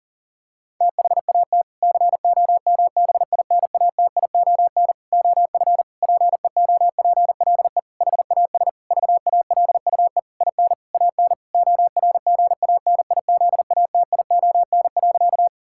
Morse code, heard as a keyed tone, about 40 wpm, 700 Hz. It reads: THAT COMBINATION OF PEOPLE HAS VALUE IN AN ORGANIZATION.
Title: Sentences from Top 1000 Words - Review Part 6 - 40wpm